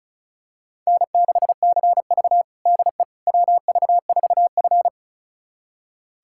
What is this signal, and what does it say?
Morse code, heard as a keyed tone, about 36 wpm, 700 Hz. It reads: N6CV DE WV4F
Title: USA Callsign DE Callsign - 4 Characters - 35wpm